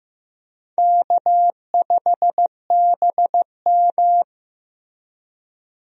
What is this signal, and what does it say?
Morse code, heard as a keyed tone, about 15 wpm, 700 Hz. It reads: K5BM
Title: USA Call Signs - 4 Characters - Repeated 3x - 15wpm